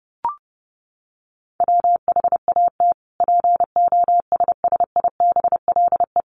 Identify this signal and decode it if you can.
Morse code, heard as a keyed tone, about 30 wpm, 700 Hz. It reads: WHAT POSSIBLE
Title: Sentences from Top 1300 Words 30wpm